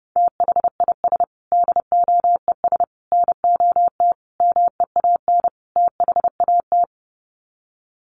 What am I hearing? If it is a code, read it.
Morse code, heard as a keyed tone, about 30 wpm, 700 Hz. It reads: THIS DOES NOT MEAN THAT
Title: Sets of 5 Words Encore2 30wpm